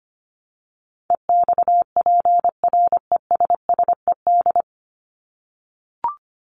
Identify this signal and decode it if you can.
Morse code, heard as a keyed tone, about 25 wpm, 700 Hz. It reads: EXPRESSED
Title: ED Suffix - Top 100 Words with Base Words 25wpm